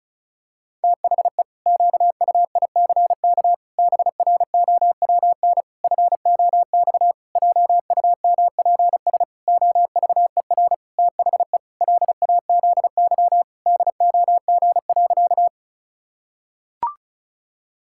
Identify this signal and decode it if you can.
Morse code, heard as a keyed tone, about 36 wpm, 700 Hz. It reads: THE QUICK BROWN FOX JUMPS OVER THE LAZY DOG.
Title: Pangram Challenge v2 - 35wpm